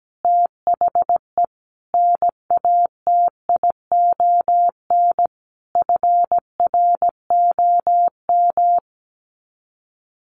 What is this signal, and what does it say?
Morse code, heard as a keyed tone, about 17 wpm, 700 Hz. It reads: THE NATION FROM
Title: Sets of 5 Words 17wpm